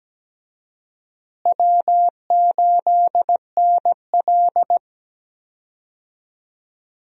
Morse code, heard as a keyed tone, about 17 wpm, 700 Hz. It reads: W8NL
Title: USA Call Signs - 4 Characters - Repeated 3x - 17wpm